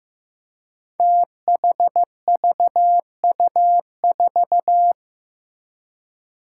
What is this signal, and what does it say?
Morse code, heard as a keyed tone, about 15 wpm, 700 Hz. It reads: THVU4
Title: Five Challenging Letter-Number Combinations 15wpm